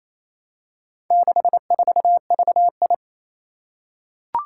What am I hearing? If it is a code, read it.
Morse code, heard as a keyed tone, about 27 wpm, 700 Hz. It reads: 64VI
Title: Four Challenging Letter-Number Combinations 28wpm